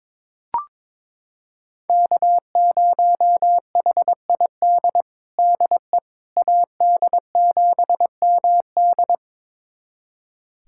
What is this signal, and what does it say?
Morse code, heard as a keyed tone, about 22 wpm, 700 Hz. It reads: K0HID DE AD7MD
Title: USA Callsign DE Callsign - 5 Characters - 22wpm